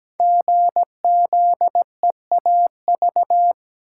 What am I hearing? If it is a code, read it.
Morse code, heard as a keyed tone, about 17 wpm, 700 Hz. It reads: GZEAV
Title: Code Groups - 5 Letter:Numbers - 17wpm